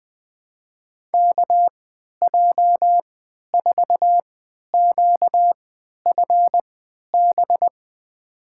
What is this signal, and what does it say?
Morse code, heard as a keyed tone, about 20 wpm, spaced as for 11 wpm, 700 Hz. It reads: KJ4QFB
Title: USA Call Signs - 6 Characters - 3x Character Spacing - 20wpm